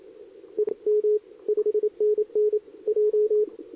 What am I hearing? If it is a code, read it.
Morse code, heard as a keyed tone, about 27 wpm, 440 Hz, with radio noise. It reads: E M 5NN J
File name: CQWWCW2023 / SP